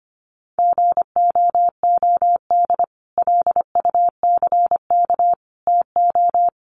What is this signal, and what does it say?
Morse code, heard as a keyed tone, about 25 wpm, 700 Hz. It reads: GOOD LUCK TO
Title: Sentences from Top 1500 Words 25wpm